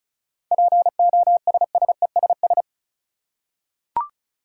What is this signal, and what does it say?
Morse code, heard as a keyed tone, about 36 wpm, 700 Hz. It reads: POSSESS
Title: Seven Letter Words - 35wpm